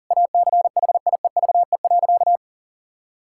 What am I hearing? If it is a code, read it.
Morse code, heard as a keyed tone, about 40 wpm, 700 Hz. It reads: ACHIEVE.
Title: Sentences from Top 1000 Words - Review Part 6 - 40wpm